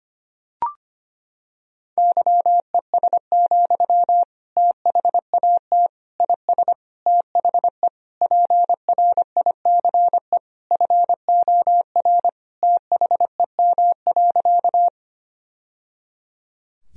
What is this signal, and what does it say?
Morse code, heard as a keyed tone, about 25 wpm, 700 Hz. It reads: YES, THAT IS THE PRICE FOR THEM.
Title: Sentences from Top 1000 Words - Review Part 4 - 25wpm